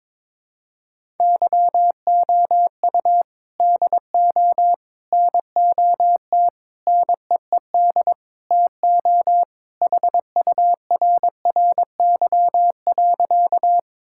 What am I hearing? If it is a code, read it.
Morse code, heard as a keyed tone, about 22 wpm, 700 Hz. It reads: YOU DO NOT NEED TO HURRY.
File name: practice / 22wpm / Sentences with 4-Word Phrases - 22wpm